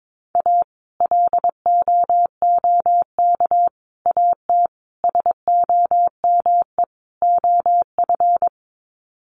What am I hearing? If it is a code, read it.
Morse code, heard as a keyed tone, about 22 wpm, 700 Hz. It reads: A LOOK AT SOME OF
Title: Sets of 5 Words Encore2 22wpm